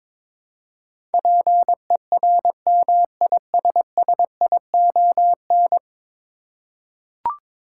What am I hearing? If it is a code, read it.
Morse code, heard as a keyed tone, about 22 wpm, 700 Hz. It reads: PERMISSION
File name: practice / 22wpm / 10 Letter Words - First 3 Letters Given - 22wpm